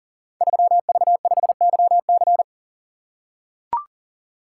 Morse code, heard as a keyed tone, about 39 wpm, 700 Hz. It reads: 3V5YC
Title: Code Groups - 5 Letter:Numbers - 40wpm